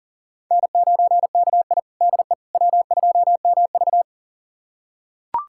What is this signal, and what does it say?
Morse code, heard as a keyed tone, about 40 wpm, 700 Hz. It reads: N9KI DE W2MV
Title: USA Callsign DE Callsign - 4 Characters - 40wpm